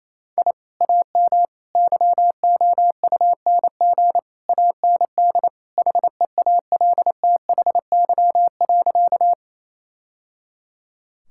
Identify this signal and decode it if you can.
Morse code, heard as a keyed tone, about 28 wpm, 700 Hz. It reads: I AM YOUNG AND HEALTHY.
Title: Sentences from Top 1000 Words - Review Part 5 - 28wpm